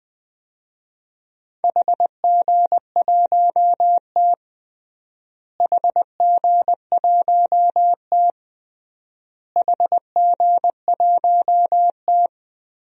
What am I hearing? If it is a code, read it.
Morse code, heard as a keyed tone, about 21 wpm, 700 Hz. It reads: HG1T HG1T HG1T
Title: DXCC-100 Call Signs - 4 Characters - Repeated 3x - 20wpm